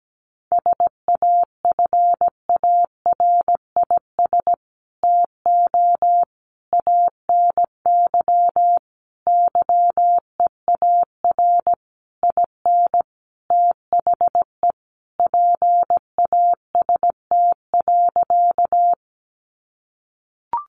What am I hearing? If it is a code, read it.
Morse code, heard as a keyed tone, about 17 wpm, 700 Hz. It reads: SAFARIS TO ANY YEAR IN THE PAST.
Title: A Sound Of Thunder 17wpm